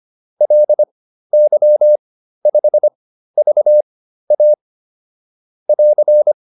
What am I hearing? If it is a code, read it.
Morse code, heard as a keyed tone, about 25 wpm, spaced as for 13 wpm, 600 Hz. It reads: LY5VA <AR>